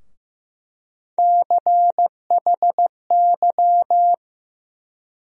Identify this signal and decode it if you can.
Morse code, heard as a keyed tone, about 15 wpm, 700 Hz. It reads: CHY
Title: USA Call Sign Postfixes - 3 Letters 15wpm